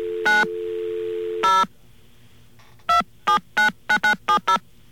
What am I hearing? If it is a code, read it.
DTMF, heard as telephone touch tones, tones of about 110 ms. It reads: 9 0 2799970